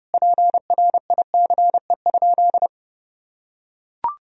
Morse code, heard as a keyed tone, about 32 wpm, 700 Hz. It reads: PRICE?
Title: Sentences from Top 1000 Words - Review Part 3 - 30wpm